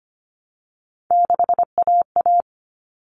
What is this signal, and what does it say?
Morse code, heard as a keyed tone, about 25 wpm, 700 Hz. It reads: 6AA
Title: All Cut Numbers - 3-Digit 25wpm